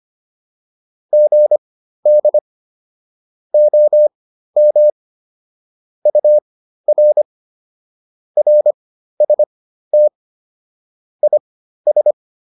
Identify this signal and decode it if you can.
Morse code, heard as a keyed tone, about 25 wpm, spaced as for 13 wpm, 600 Hz. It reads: GD OM UR RST IS